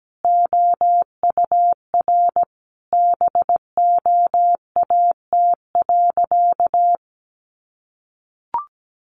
Morse code, heard as a keyed tone, about 17 wpm, 700 Hz. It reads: OUR BOAT.
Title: Sentences from Top 1400 Words 17wpm